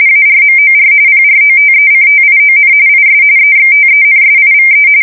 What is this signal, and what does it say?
RTTY, about 45.45 baud. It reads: ARMOR PROTECTION IS EXTENSIVE
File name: fsk170 / battleshipnj